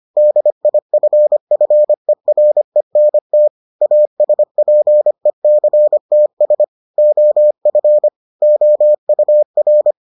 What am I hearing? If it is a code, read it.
Morse code, heard as a keyed tone, about 25 wpm, 600 Hz. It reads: DIFFERENT ASPECTS OF OUR